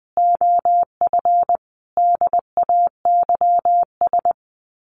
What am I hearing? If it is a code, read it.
Morse code, heard as a keyed tone, about 20 wpm, 700 Hz. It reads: OF DAYS
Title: Sets of 5 Words Encore2 20wpm